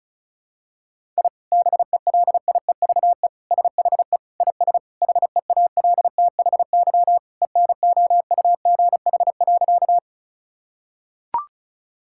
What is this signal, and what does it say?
Morse code, heard as a keyed tone, about 36 wpm, 700 Hz. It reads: I BELIEVE SHE IS HEALTHY ENOUGH.
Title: Sentences from Top 1000 Words - Review Part 5 - 35wpm